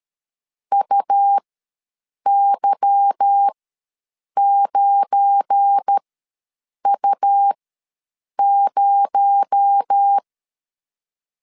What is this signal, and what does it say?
Morse code, heard as a keyed tone, about 13 wpm, spaced as for 7 wpm, 800 Hz. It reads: UY9U0
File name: LEZIONI / 8 YJ45 / 8 Y ws 12wpm 3409